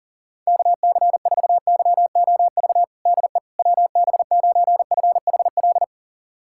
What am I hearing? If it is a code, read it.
Morse code, heard as a keyed tone, about 40 wpm, 700 Hz. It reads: KC4YOV DE WB9FHL
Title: USA Callsign DE Callsign - 6 Characters - 40wpm